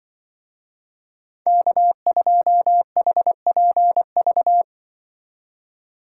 Morse code, heard as a keyed tone, about 25 wpm, 700 Hz. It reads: K2HPV
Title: Speed Racing - USA 1x3 Call Signs 20wpm